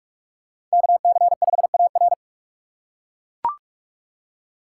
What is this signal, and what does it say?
Morse code, heard as a keyed tone, about 40 wpm, 700 Hz. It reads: KC5AR